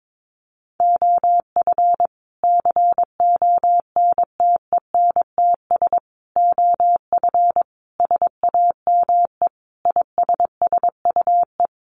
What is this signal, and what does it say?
Morse code, heard as a keyed tone, about 22 wpm, 700 Hz. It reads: OF CONTENTS OF SAME ISSUE
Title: Sets of 5 Words Encore2 22wpm